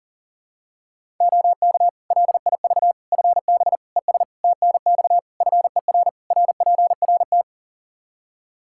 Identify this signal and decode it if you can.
Morse code, heard as a keyed tone, about 40 wpm, 700 Hz. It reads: OK LIV FB ES TNX FER RPRT
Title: Practice QSO - Exchange 2 - Info - 40wpm